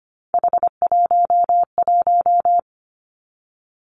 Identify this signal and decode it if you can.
Morse code, heard as a keyed tone, about 25 wpm, 700 Hz. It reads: H11
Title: Three Challenging Letter-Number Combinations 25wpm